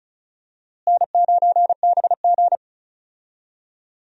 Morse code, heard as a keyed tone, about 36 wpm, 700 Hz. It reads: N9BG
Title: USA Call Signs - 4 Characters - Repeated 3x - 35wpm